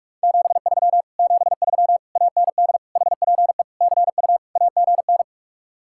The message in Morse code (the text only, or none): 73 73 AND HPE CU AGN